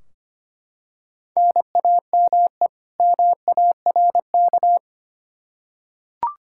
Morse code, heard as a keyed tone, about 25 wpm, 700 Hz. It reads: NAME MARK
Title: Practice QSO - Exchange 1 - Beginning - 25wpm